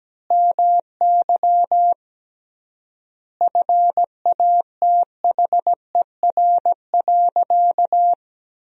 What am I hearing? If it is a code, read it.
Morse code, heard as a keyed tone, about 17 wpm, 700 Hz. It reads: MY FATHER.
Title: 3x Word-Spacing - Sentences from Top 400 Words - 17wpm